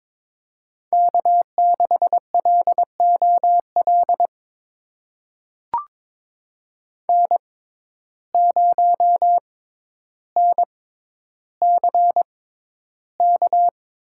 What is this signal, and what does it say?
Morse code, heard as a keyed tone, about 23 wpm, spaced as for 11 wpm, 700 Hz. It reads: K6LOL N0NCK
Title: USA Call Signs - 5 Characters - 6x Character Spacing - 22wpm